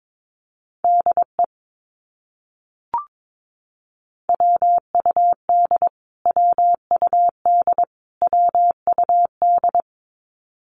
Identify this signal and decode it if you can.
Morse code, heard as a keyed tone, about 22 wpm, 700 Hz. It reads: DE WUD WUD WUD
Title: Abbreviations Repeated 3x - 22wpm